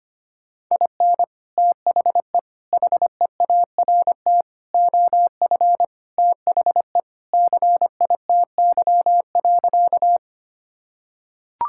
Morse code, heard as a keyed tone, about 25 wpm, 700 Hz. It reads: IN THE HEART OF THE CITY.
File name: practice / 25wpm / Sentences with 3-Word Phrases - 25wpm